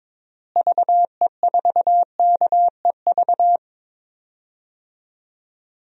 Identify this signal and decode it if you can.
Morse code, heard as a keyed tone, about 22 wpm, 700 Hz. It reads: VE4KEV
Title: DXCC-100 Call Signs - 6 Characters - Repeated 3x - 22wpm